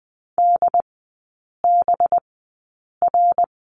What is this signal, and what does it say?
Morse code, heard as a keyed tone, about 20 wpm, spaced as for 8 wpm, 700 Hz. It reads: DBR
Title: 184 Lesson 30e - Letter b - Three Characters 20wpm